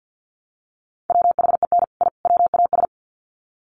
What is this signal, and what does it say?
Morse code, heard as a keyed tone, about 50 wpm, 700 Hz. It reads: WHEN I WAS
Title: Sets of 3 Words 50wpm